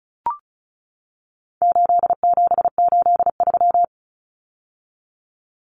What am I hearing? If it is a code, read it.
Morse code, heard as a keyed tone, about 34 wpm, 700 Hz. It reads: 8783
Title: Numbers - 4-Digit 35wpm